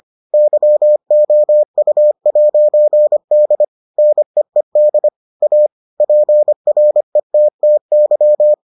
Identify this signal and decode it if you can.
Morse code, heard as a keyed tone, about 25 wpm, 600 Hz. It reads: YOU'D NEED A PRETTY